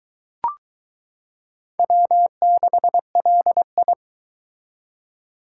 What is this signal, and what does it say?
Morse code, heard as a keyed tone, about 22 wpm, 700 Hz. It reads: W6LI